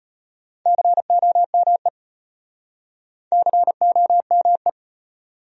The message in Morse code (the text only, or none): COME COME